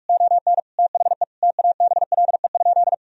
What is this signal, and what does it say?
Morse code, heard as a keyed tone, about 44 wpm, 700 Hz. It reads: ON THE TABLE?
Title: Sentences from Top 1000 Words - Review Part 3 - 45wpm